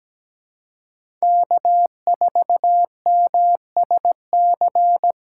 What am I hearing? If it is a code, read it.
Morse code, heard as a keyed tone, about 17 wpm, 700 Hz. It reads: K4MSC
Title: USA Call Signs - 3x Character Spacing - 17wpm